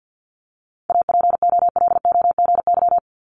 Morse code, heard as a keyed tone, about 50 wpm, 700 Hz. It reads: APOLOGY
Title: Multi-letter Phonemes 50wpm